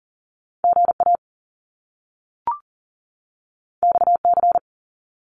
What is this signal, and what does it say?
Morse code, heard as a keyed tone, about 38 wpm, 700 Hz. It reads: GA XC
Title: Top Two Letter Combinations in Words 40wpm